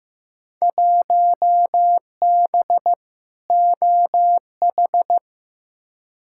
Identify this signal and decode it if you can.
Morse code, heard as a keyed tone, about 15 wpm, 700 Hz. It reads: 1B OH
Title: Field Day Exchanges - 15wpm